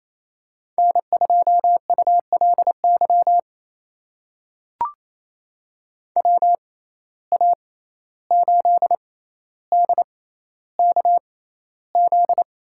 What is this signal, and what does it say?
Morse code, heard as a keyed tone, about 28 wpm, spaced as for 14 wpm, 700 Hz. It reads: N2ULY WA8DKZ
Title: USA Call Signs - 6x Character Spacing - 28wpm